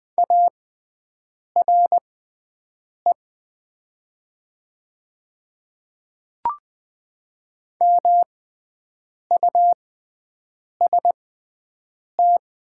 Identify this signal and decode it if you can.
Morse code, heard as a keyed tone, about 20 wpm, spaced as for 7 wpm, 700 Hz. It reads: ARE MUST
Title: 6x Character Spacing - Top 500 Words - 20wpm